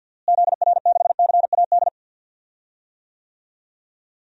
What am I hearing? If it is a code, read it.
Morse code, heard as a keyed tone, about 51 wpm, 700 Hz. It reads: ZR6CAD
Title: DXCC-100 Call Signs - 6 Characters - Repeated 2x - 50wpm